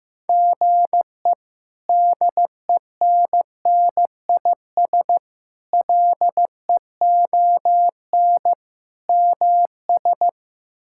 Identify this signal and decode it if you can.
Morse code, heard as a keyed tone, about 15 wpm, 700 Hz. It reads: GE DENNIS LEON MS
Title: K1USN SST QSO Practice - 15wpm